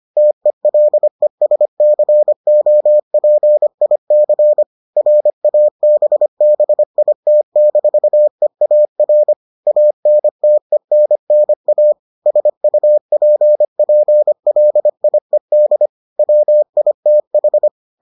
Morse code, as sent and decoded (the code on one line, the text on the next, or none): - . .-.. ... -.-. --- .--. .. -.-. / .-. .- -... -... .. - -....- . .- .-. / .- -. - . -. -. .- / ... ..- .--. .--. .-.. .. -.. / .-- .. - ....
TELESCOPIC RABBIT-EAR ANTENNA SUPPLIED WITH